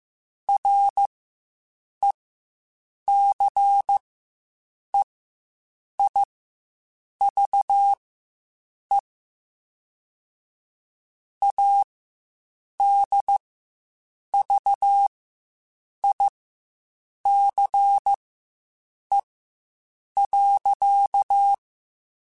Morse code, heard as a keyed tone, about 15 wpm, spaced as for 7 wpm, 800 Hz. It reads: RECEIVE ADVICE.
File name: Quotes Spaced / Quotes Spaced-mp3s / Advice sp @10